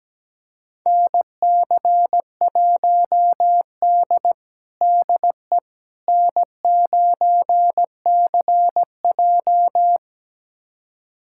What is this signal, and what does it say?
Morse code, heard as a keyed tone, about 17 wpm, 700 Hz. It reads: NC1D DE N9CJ
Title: USA Callsign DE Callsign - 4 Characters - 17wpm